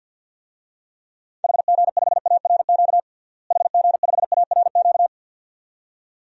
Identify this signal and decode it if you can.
Morse code, heard as a keyed tone, about 49 wpm, 700 Hz. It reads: HG5ARQ HG5ARQ
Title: DXCC-100 Call Signs - 6 Characters - Repeated 2x - 50wpm